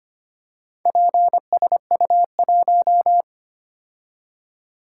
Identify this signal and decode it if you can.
Morse code, heard as a keyed tone, about 25 wpm, 700 Hz. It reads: PSU1
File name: practice / 25wpm / Four Letter-Number Combinations - 25wpm